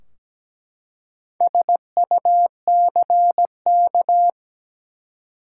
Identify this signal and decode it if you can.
Morse code, heard as a keyed tone, about 17 wpm, 700 Hz. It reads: SUCK
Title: Four Letter Words - 17wpm